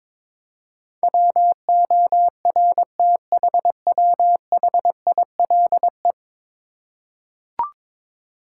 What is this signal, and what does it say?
Morse code, heard as a keyed tone, about 22 wpm, 700 Hz. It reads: WORTHWHILE
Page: morse-code-ninja-practice-22wpm